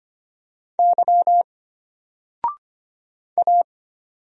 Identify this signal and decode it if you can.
Morse code, heard as a keyed tone, about 25 wpm, 700 Hz. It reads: Y A
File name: Single-Letter-Number-25wpm